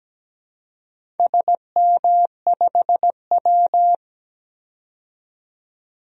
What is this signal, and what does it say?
Morse code, heard as a keyed tone, about 17 wpm, 700 Hz. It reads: SM5W
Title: DXCC-100 Call Signs - 4 Characters - 17wpm